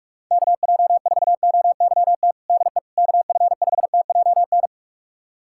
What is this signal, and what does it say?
Morse code, heard as a keyed tone, about 48 wpm, 700 Hz. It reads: KJ4OYT DE KF5TJN
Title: USA Callsign DE Callsign - 6 Characters - 45wpm